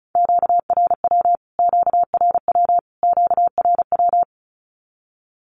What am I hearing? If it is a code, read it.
Morse code, heard as a keyed tone, about 34 wpm, 700 Hz. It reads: QRW QRW QRW
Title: Q-Codes Repeated 3x 35wpm